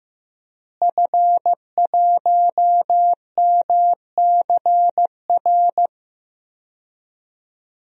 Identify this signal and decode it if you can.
Morse code, heard as a keyed tone, about 15 wpm, 700 Hz. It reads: F1MCR
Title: DXCC-100 Call Signs - 5 Characters - Repeated 2x - 15wpm